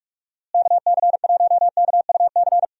Morse code, heard as a keyed tone, about 44 wpm, 700 Hz. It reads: KC1KUC
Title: USA Call Signs - 6 Characters - Repeated 3x - 45wpm